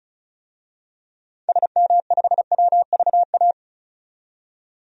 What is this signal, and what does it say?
Morse code, heard as a keyed tone, about 36 wpm, 700 Hz. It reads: SM5WVA
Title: DXCC-100 Call Signs - 6 Characters - 35wpm